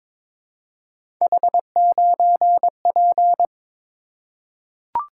Morse code, heard as a keyed tone, about 22 wpm, 700 Hz. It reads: H9P
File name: practice / 22wpm / Three Letter-Number Combinations - 22wpm